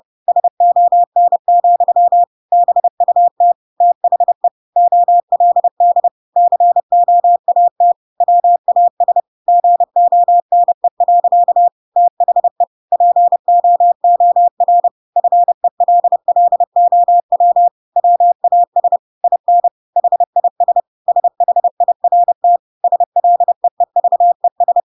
Morse code, heard as a keyed tone, about 30 wpm, 700 Hz. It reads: SON, BUT THE OLD COAT WAS GONE. THE POOR FELLOW WAS IN HIS SHIRT SLEEVES
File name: Pinocchio / Pinocchio ch8 30 30 ik1hge